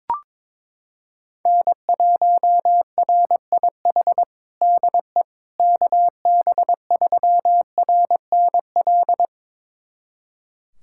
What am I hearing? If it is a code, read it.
Morse code, heard as a keyed tone, about 22 wpm, 700 Hz. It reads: N1RIH DE KB3RNL
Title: USA Callsign DE Callsign 22wpm